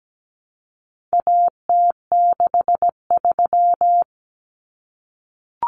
Morse code, heard as a keyed tone, about 17 wpm, 700 Hz. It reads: AT63
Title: Common Cut Numbers - 4-Digit 17wpm